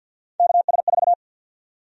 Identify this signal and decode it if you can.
Morse code, heard as a keyed tone, about 48 wpm, 700 Hz. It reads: KS4
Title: USA Call Sign Prefixes - Repeated 2x - 50wpm